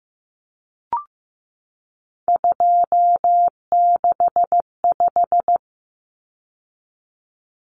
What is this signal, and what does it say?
Morse code, heard as a keyed tone, about 15 wpm, 700 Hz. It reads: 265
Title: Numbers - 3-Digit 15wpm